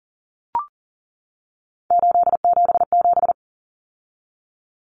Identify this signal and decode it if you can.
Morse code, heard as a keyed tone, about 40 wpm, 700 Hz. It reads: 877
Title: Common Cut Numbers - 3-Digit 40wpm